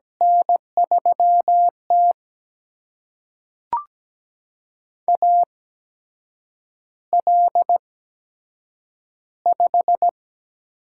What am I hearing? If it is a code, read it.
Morse code, heard as a keyed tone, about 17 wpm, spaced as for 7 wpm, 700 Hz. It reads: N3T AL5
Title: USA Call Signs - 4 Characters - 8x Character Spacing - 17wpm